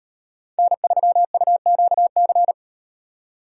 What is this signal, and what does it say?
Morse code, heard as a keyed tone, about 35 wpm, 700 Hz. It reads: N3UQC